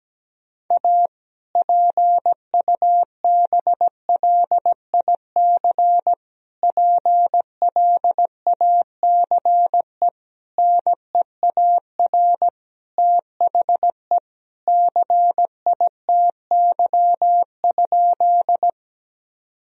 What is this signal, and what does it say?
Morse code, heard as a keyed tone, about 17 wpm, 700 Hz. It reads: A PUBLIC PLACE NEAR THE CITY?
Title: Sentences from Top 1000 Words - Review Part 3 - 17wpm